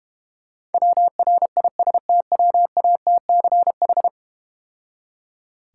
Morse code, heard as a keyed tone, about 32 wpm, 700 Hz. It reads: WRISTWATCH